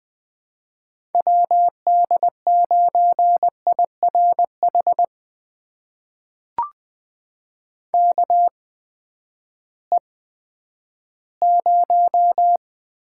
Morse code, heard as a keyed tone, about 21 wpm, spaced as for 11 wpm, 700 Hz. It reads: WD9IRH KE0
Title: USA Call Signs - 6 Characters - 8x Character Spacing - 20wpm